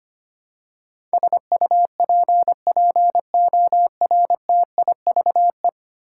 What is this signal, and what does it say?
Morse code, heard as a keyed tone, about 25 wpm, 700 Hz. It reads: SUPPORTIVE
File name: 10 Letter Words - First 3 Letters Given - 25wpm